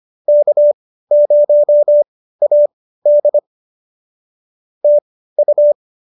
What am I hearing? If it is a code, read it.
Morse code, heard as a keyed tone, about 25 wpm, spaced as for 15 wpm, 590 Hz. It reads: K0AD TU